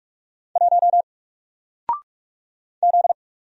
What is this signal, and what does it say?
Morse code, heard as a keyed tone, about 45 wpm, 700 Hz. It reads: 1 Z
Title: Single Letter-Number 45wpm